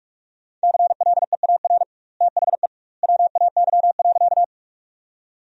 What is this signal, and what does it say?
Morse code, heard as a keyed tone, about 46 wpm, 700 Hz. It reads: CLEAR THE WAY.
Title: 2x Word-Spacing - Sentences from Top 500 Words - 45wpm